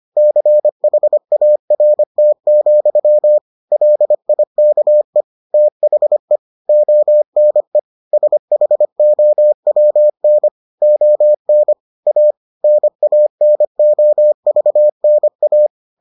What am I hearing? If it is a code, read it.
Morse code, heard as a keyed tone, about 25 wpm, 600 Hz. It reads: CHART, LIKE THE ONE SHOWN ON A NANOVNA